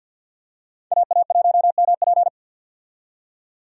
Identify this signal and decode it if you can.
Morse code, heard as a keyed tone, about 48 wpm, 700 Hz. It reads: AA1MP
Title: USA Call Signs - 5 Characters - 50wpm